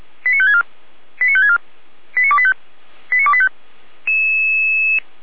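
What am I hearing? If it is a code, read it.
Five-tone selective call (ZVEI), heard as quick runs of five tones, tones of about 70 ms. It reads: 89564 89564 89285 89285
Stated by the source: ZVEI five-tone selective call, code 89270, received on an SDR.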